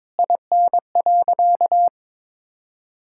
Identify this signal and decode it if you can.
Morse code, heard as a keyed tone, about 22 wpm, 700 Hz. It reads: IN.
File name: practice / 22wpm / Sentences with 2-Word Phrases - 22wpm